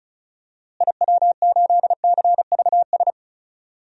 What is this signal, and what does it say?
Morse code, heard as a keyed tone, about 36 wpm, 700 Hz. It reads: IW8CVS
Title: DXCC-100 Call Signs - 6 Characters - Repeated 2x - 35wpm